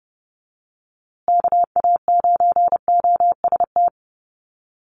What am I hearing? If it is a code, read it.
Morse code, heard as a keyed tone, about 30 wpm, 700 Hz. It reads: KA9OST
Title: USA Call Signs Repeated 2x 30wpm